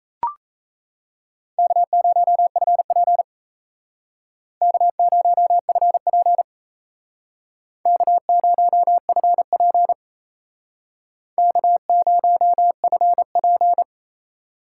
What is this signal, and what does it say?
Morse code, heard as a keyed tone, about 35 wpm, 700 Hz. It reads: K0FP K0FP K0FP K0FP
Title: Speed Racing - USA 1x2 Call Signs 28wpm